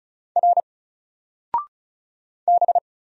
Morse code, heard as a keyed tone, about 36 wpm, 700 Hz. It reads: R B
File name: Single-Letter-Number-35wpm